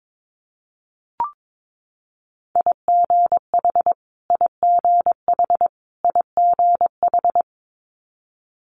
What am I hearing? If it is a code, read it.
Morse code, heard as a keyed tone, about 22 wpm, 700 Hz. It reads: IGH IGH IGH
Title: Multi-letter Phonemes Repeated 3x 22wpm